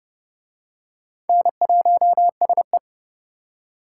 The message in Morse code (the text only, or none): N1SE